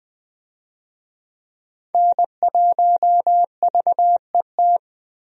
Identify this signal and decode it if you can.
Morse code, heard as a keyed tone, about 20 wpm, 700 Hz. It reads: N1VET
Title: USA Call Signs - 5 Characters - Repeated 3x - 20wpm